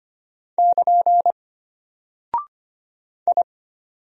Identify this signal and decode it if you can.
Morse code, heard as a keyed tone, about 25 wpm, 700 Hz. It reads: <KN> I
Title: Single Character (Letter:Number:Punctuation:Prosign) Alt 25wpm